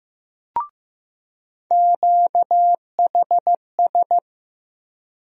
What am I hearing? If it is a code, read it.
Morse code, heard as a keyed tone, about 15 wpm, 700 Hz. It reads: QHS
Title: USA Call Sign Postfixes - 3 Letters 15wpm